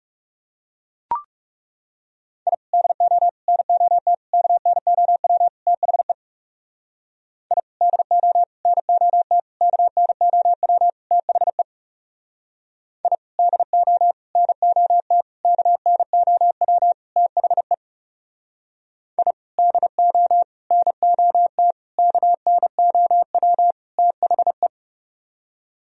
Morse code, heard as a keyed tone, about 37 wpm, 700 Hz. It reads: I DO NOT KNOW THE I DO NOT KNOW THE I DO NOT KNOW THE I DO NOT KNOW THE
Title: Speed-Racing - Sets of 5 Words 30wpm